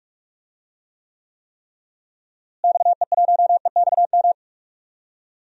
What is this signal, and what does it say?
Morse code, heard as a keyed tone, about 41 wpm, 700 Hz. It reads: XE1EXM
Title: DXCC-100 Call Signs - 6 Characters - Repeated 3x - 45wpm